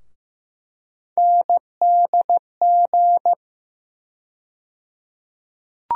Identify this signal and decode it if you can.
Morse code, heard as a keyed tone, about 15 wpm, 700 Hz. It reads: NDG
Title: USA Call Sign Postfixes - 3 Letters 15wpm